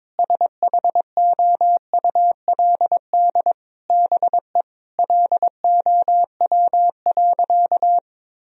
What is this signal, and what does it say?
Morse code, heard as a keyed tone, about 22 wpm, 700 Hz. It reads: SHOULD BE LOW.
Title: Sentences from Top 1000 Words - Review Part 1 - 22wpm